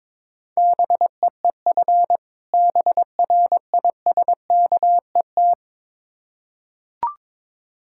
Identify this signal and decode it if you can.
Morse code, heard as a keyed tone, about 22 wpm, 700 Hz. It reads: BEEF BRISKET
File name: Difficult Words with Hints - 22wpm